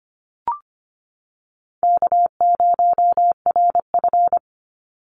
Morse code, heard as a keyed tone, about 25 wpm, 700 Hz. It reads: K0RF
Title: Call Signs - 25wpm